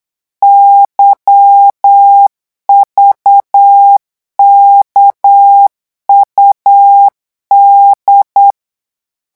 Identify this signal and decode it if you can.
Morse code, heard as a keyed tone, about 8 wpm, 800 Hz. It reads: YVKUD
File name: morse mp3 / revision3-vitesse 8 mots